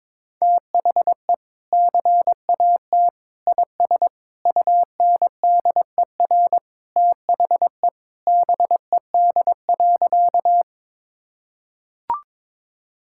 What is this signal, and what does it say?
Morse code, heard as a keyed tone, about 23 wpm, 700 Hz. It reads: THE CAT IS UNDER THE BED.
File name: practice / 22wpm / Sentences with 2-Word Phrases - 22wpm